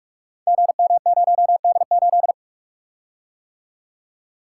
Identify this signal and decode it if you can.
Morse code, heard as a keyed tone, about 45 wpm, 700 Hz. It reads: GM0D8
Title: Code Groups - 5 Letter:Numbers - 45wpm